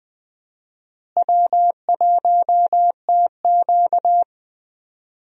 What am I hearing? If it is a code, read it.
Morse code, heard as a keyed tone, about 20 wpm, 700 Hz. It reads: W1TQ
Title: USA Call Signs - 4 Characters - Repeated 2x - 20wpm